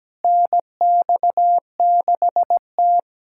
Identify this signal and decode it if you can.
Morse code, heard as a keyed tone, about 17 wpm, 700 Hz. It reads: NX6T
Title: USA Callsign DE Callsign - 4 Characters - 17wpm